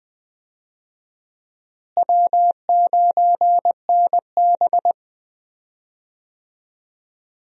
Morse code, heard as a keyed tone, about 21 wpm, 700 Hz. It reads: W9NB
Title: USA Call Signs - 4 Characters - Repeated 2x - 20wpm